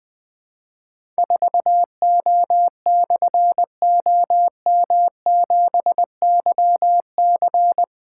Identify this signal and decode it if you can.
Morse code, heard as a keyed tone, about 20 wpm, 700 Hz. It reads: 4O/OM7YC
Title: Difficult Call Signs 20wpm